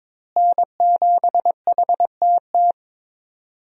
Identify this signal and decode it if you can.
Morse code, heard as a keyed tone, about 21 wpm, 700 Hz. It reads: N7HTT
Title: USA Call Signs - 5 Characters - Repeated 2x - 22wpm